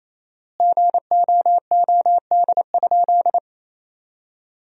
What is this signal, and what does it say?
Morse code, heard as a keyed tone, about 28 wpm, 700 Hz. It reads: GOOD?
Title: 4x Word-Spacing - Sentences from Top 200 Words - 28wpm